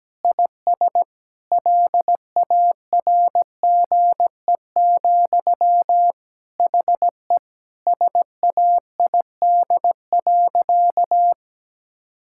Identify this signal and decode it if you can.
Morse code, heard as a keyed tone, about 17 wpm, 700 Hz. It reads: IS LARGE, HE SAID.
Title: Sentences from Top 1600 Words 17wpm